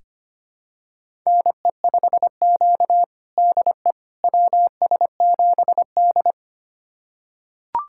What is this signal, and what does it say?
Morse code, heard as a keyed tone, about 25 wpm, 700 Hz. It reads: NE5Q DE WS7D
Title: USA Callsign DE Callsign - 4 Characters - 25wpm